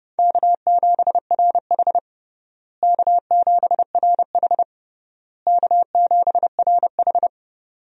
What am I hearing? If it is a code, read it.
Morse code, heard as a keyed tone, about 29 wpm, 700 Hz. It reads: K7RH K7RH K7RH
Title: USA Call Signs - 4 Characters - Repeated 3x - 30wpm